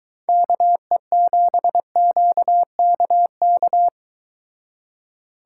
Morse code, heard as a keyed tone, about 23 wpm, 700 Hz. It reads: KE7QKK